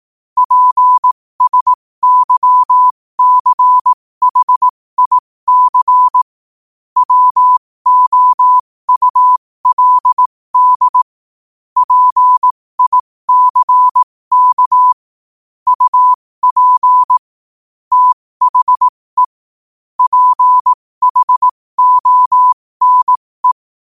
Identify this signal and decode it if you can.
Morse code, heard as a keyed tone, about 18 wpm, spaced as for 15 wpm, 1000 Hz. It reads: PSYCHIC WOULD PICK UP THE PHONE
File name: qotdfiles / 15wpm